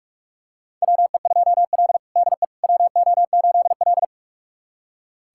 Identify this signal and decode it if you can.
Morse code, heard as a keyed tone, about 42 wpm, 700 Hz. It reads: WE2L DE WO8L
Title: USA Callsign DE Callsign - 4 Characters - 45wpm